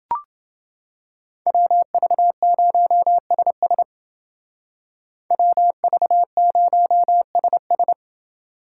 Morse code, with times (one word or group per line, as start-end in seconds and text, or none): 1.5-3.8 WV0SS
5.3-7.9 WV0SS